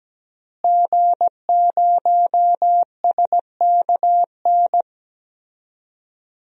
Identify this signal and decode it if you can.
Morse code, heard as a keyed tone, about 17 wpm, 700 Hz. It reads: G0SKN
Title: DXCC-100 Call Signs - 5 Characters - Repeated 2x - 17wpm